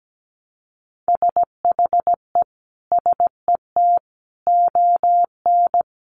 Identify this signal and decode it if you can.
Morse code, heard as a keyed tone, about 17 wpm, 700 Hz. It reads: SHE SET ON
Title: Sentences from Top 1300 Words 17wpm